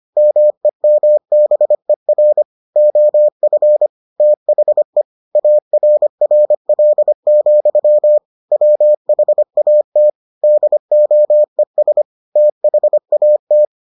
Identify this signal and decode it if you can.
Morse code, heard as a keyed tone, about 25 wpm, 600 Hz. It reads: MEMBER OF THE ARRL, WHAT DOES THAT